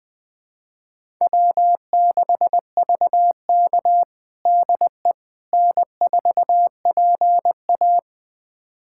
Morse code, heard as a keyed tone, about 20 wpm, 700 Hz. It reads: W6VK DE N4PA
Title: USA Callsign DE Callsign - 4 Characters - 20wpm